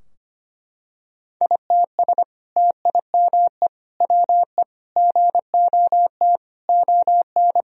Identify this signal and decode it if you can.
Morse code, heard as a keyed tone, about 25 wpm, 700 Hz. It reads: ITS TIME WE GOT ON
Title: Sentences from Top 1000 Words - Review Part 4 - 25wpm